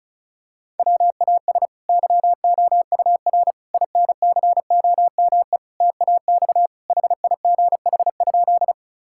Morse code, heard as a keyed tone, about 36 wpm, 700 Hz. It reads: WAS YOUR INCOME TAX HIGH?
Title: Sentences from Top 1000 Words - Review Part 5 - 35wpm